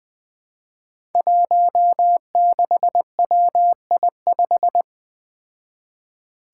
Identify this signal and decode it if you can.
Morse code, heard as a keyed tone, about 20 wpm, 700 Hz. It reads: 16WI5
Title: Five Challenging Letter-Number Combinations 20wpm